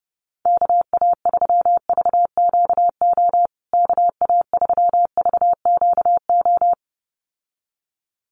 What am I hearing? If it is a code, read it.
Morse code, heard as a keyed tone, about 30 wpm, 700 Hz. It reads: KA3VQO KA3VQO
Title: USA Call Signs Repeated 2x 30wpm